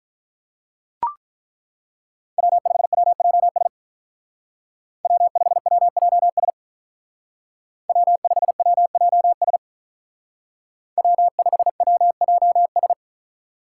Morse code, heard as a keyed tone, about 43 wpm, 700 Hz. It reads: W5WJS W5WJS W5WJS W5WJS
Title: Speed Racing - USA 1x3 Call Signs 35wpm